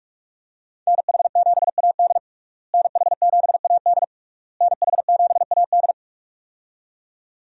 Morse code, heard as a keyed tone, about 46 wpm, 700 Hz. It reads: NH7AD NH7AD NH7AD
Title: USA Call Signs - 5 Characters - Repeated 3x - 45wpm